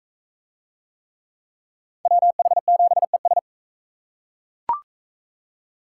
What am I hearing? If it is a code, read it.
Morse code, heard as a keyed tone, about 38 wpm, 700 Hz. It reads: WH7ES